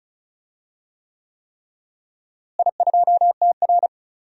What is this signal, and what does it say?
Morse code, heard as a keyed tone, about 36 wpm, 700 Hz. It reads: I2TR
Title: DXCC-100 Call Signs - 4 Characters - Repeated 3x - 35wpm